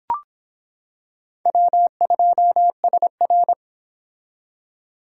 Morse code, heard as a keyed tone, about 25 wpm, 700 Hz. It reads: W2SR